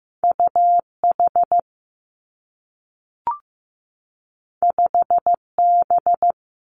Morse code, heard as a keyed tone, about 15 wpm, 700 Hz. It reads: U H 5 B
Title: Two Challenging Letter-Number Combinations 15wpm